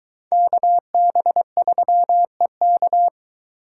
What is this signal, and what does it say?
Morse code, heard as a keyed tone, about 22 wpm, 700 Hz. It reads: KB3EK